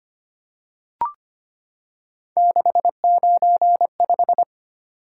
Morse code, heard as a keyed tone, about 25 wpm, 700 Hz. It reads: 695
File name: Three Letter-Number Combinations - 25wpm